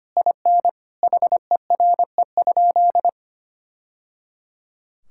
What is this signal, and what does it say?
Morse code, heard as a keyed tone, about 25 wpm, 700 Hz. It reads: IN HERE?
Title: Sentences from Top 1000 Words - Review Part 6 - 25wpm